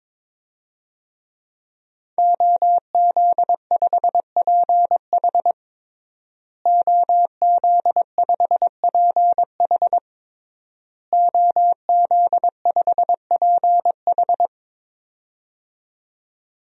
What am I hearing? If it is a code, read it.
Morse code, heard as a keyed tone, about 22 wpm, 700 Hz. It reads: OZ5PH OZ5PH OZ5PH
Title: DXCC-100 Call Signs - 5 Characters - Repeated 3x - 22wpm